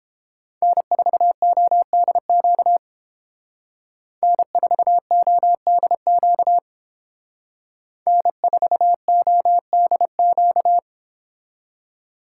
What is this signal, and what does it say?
Morse code, heard as a keyed tone, about 30 wpm, 700 Hz. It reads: N4ODQ N4ODQ N4ODQ